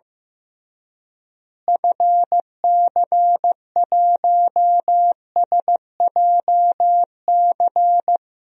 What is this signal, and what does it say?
Morse code, heard as a keyed tone, about 15 wpm, 700 Hz. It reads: FC1SJC
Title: DXCC-100 Call Signs - 6 Characters - Repeated 3x - 15wpm